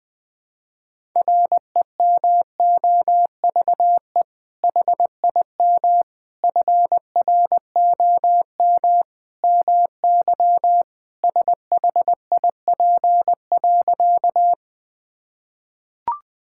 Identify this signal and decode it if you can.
Morse code, heard as a keyed tone, about 21 wpm, 700 Hz. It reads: REMOVE HIM FROM MY SHIP.
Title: Sentences from Top 1000 Words - Review Part 5 - 20wpm